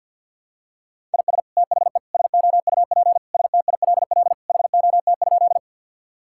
Morse code, heard as a keyed tone, about 50 wpm, 700 Hz. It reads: IS THE SOUP STILL HOT?
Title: Sentences from Top 1000 Words - Review Part 6 - 50wpm